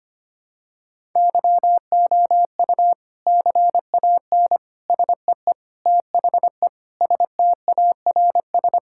Morse code, heard as a keyed tone, about 25 wpm, 700 Hz. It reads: YOU CAN SEE THE STARS
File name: practice / 25wpm / Sentences with 3-Word Phrases - 25wpm